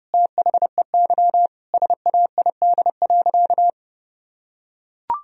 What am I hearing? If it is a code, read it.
Morse code, heard as a keyed tone, about 30 wpm, 700 Hz. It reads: THEY SAID.
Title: Sentences from Top 1000 Words - Review Part 2 - 30wpm